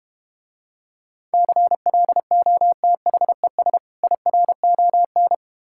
Morse code, heard as a keyed tone, about 32 wpm, 700 Hz. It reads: CLOTHES IRON